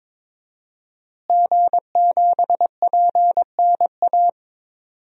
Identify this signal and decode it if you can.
Morse code, heard as a keyed tone, about 22 wpm, 700 Hz. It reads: G7PNA
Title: DXCC-100 Call Signs - 5 Characters - Repeated 3x - 22wpm